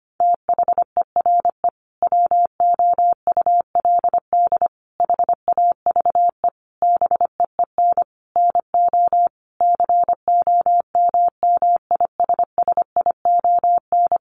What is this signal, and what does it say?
Morse code, heard as a keyed tone, about 25 wpm, 700 Hz. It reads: THERE WOULD HAVE BEEN NO COMMISSION
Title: Sentences from Top 1400 Words 25wpm